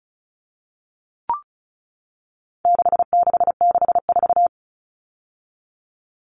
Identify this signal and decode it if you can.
Morse code, heard as a keyed tone, about 34 wpm, 700 Hz. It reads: 6664
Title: Numbers - 4-Digit 35wpm